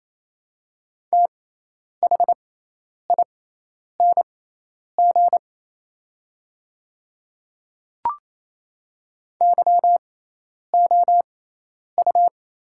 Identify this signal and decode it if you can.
Morse code, heard as a keyed tone, about 28 wpm, spaced as for 10 wpm, 700 Hz. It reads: THING YOU
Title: 6x Character Spacing - Top 100 Words - 28wpm